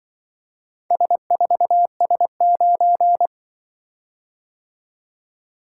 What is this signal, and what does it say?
Morse code, heard as a keyed tone, about 24 wpm, 700 Hz. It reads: S4S9
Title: Speed-Racing - Four Letter-Number Combinations - 20wpm